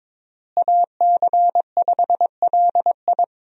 Morse code, heard as a keyed tone, about 22 wpm, 700 Hz. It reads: AC5LI
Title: USA Call Signs - 5 Characters - Repeated 2x - 22wpm